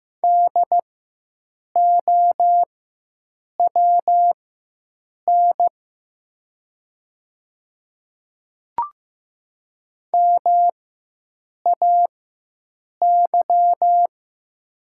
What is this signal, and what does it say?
Morse code, heard as a keyed tone, about 15 wpm, spaced as for 7 wpm, 700 Hz. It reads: DOWN MAY
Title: 4x Character Spacing - Top 200 Words - 15wpm